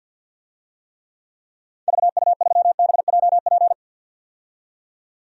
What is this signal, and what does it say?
Morse code, heard as a keyed tone, about 52 wpm, 700 Hz. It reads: VU3BJP
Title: DXCC-100 Call Signs - 6 Characters - 50wpm